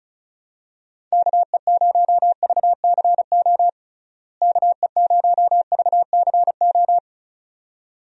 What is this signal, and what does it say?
Morse code, heard as a keyed tone, about 36 wpm, 700 Hz. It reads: KE0VCO KE0VCO
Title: USA Call Signs - 6 Characters - Repeated 2x - 35wpm